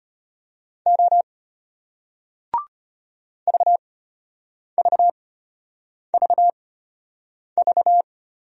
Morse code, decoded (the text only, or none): O VVVV